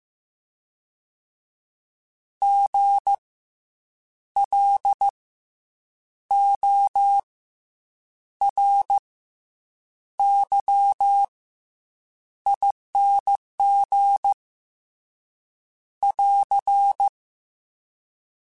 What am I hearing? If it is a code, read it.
Morse code, heard as a keyed tone, about 15 wpm, 800 Hz. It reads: G L O R Y ING <AR>